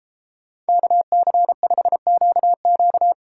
Morse code, heard as a keyed tone, about 31 wpm, 700 Hz. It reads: KC5QQ